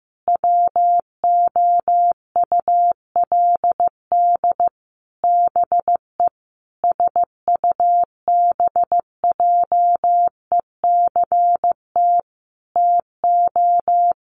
Morse code, heard as a keyed tone, about 15 wpm, 700 Hz. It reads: WOULD BE SUBJECT TO
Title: Sentences from Top 1300 Words 15wpm